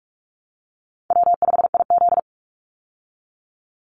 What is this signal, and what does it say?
Morse code, heard as a keyed tone, about 44 wpm, 700 Hz. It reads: W5IZ
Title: Easy Call Signs Repeated 3x - 45wpm